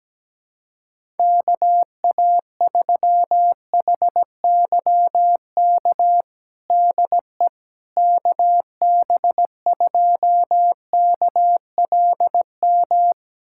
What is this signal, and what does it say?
Morse code, heard as a keyed tone, about 17 wpm, 700 Hz. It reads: KA3HYK DE KB2KLM
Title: USA Callsign DE Callsign - 6 Characters - 17wpm